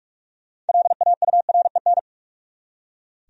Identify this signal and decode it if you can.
Morse code, heard as a keyed tone, about 44 wpm, 700 Hz. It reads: LAUREN
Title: USA Common Names - 45wpm